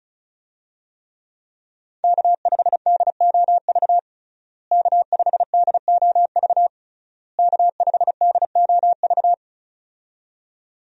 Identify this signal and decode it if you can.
Morse code, heard as a keyed tone, about 36 wpm, 700 Hz. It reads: K5DOV K5DOV K5DOV
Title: USA Call Signs - 5 Characters - Repeated 3x - 35wpm